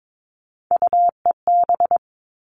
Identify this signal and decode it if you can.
Morse code, heard as a keyed tone, about 22 wpm, 700 Hz. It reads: UEB
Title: All Cut Numbers - 3-Digit 22wpm